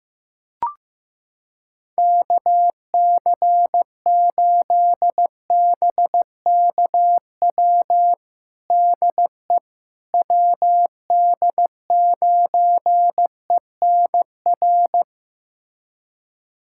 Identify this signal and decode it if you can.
Morse code, heard as a keyed tone, about 15 wpm, 700 Hz. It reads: KC8BKW DE WD9ENR
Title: USA Callsign DE Callsign - 6 Characters - 15wpm